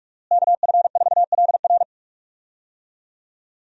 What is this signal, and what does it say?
Morse code, heard as a keyed tone, about 42 wpm, 700 Hz. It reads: KF4LR
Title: USA Call Signs - 5 Characters - Repeated 3x - 45wpm